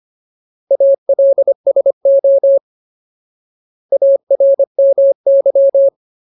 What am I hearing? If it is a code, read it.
Morse code, heard as a keyed tone, about 25 wpm, 550 Hz. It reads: ALSO ARMY
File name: Ultimate Word List / 25WPM Ultimate